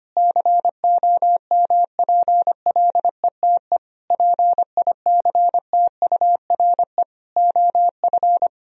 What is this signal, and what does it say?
Morse code, heard as a keyed tone, about 25 wpm, 700 Hz. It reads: COMPLETE PICTURE OF
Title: Sentences from Top 1000 Words - Review Part 2 - 25wpm